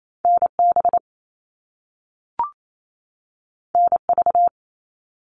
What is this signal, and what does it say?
Morse code, heard as a keyed tone, about 28 wpm, 700 Hz. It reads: NB NV
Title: All Cut Numbers - 2-Digit 28wpm